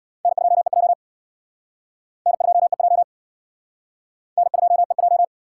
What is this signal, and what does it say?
Morse code, heard as a keyed tone, about 60 wpm, 700 Hz. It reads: N2EJ N2EJ N2EJ
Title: Speed Racing - USA 1x2 Call Signs 50wpm